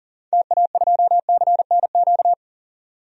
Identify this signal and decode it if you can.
Morse code, heard as a keyed tone, about 40 wpm, 700 Hz. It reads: TA2CNQ
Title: DXCC-100 Call Signs - 6 Characters - 40wpm